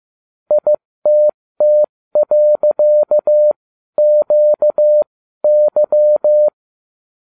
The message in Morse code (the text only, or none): ITT.QY